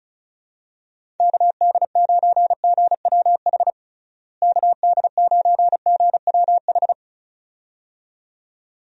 Morse code, heard as a keyed tone, about 36 wpm, 700 Hz. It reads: KD9GWH KD9GWH
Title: USA Call Signs - 6 Characters - Repeated 2x - 35wpm